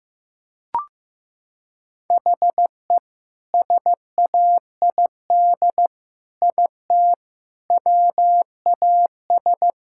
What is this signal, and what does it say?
Morse code, heard as a keyed tone, about 15 wpm, 700 Hz. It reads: HE SAID IT WAS
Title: Sentences from Top 1600 Words 15wpm